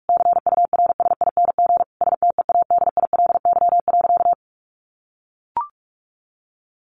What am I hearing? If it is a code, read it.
Morse code, heard as a keyed tone, about 44 wpm, 700 Hz. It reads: CURSING STEADILY.
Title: A Sound Of Thunder 45wpm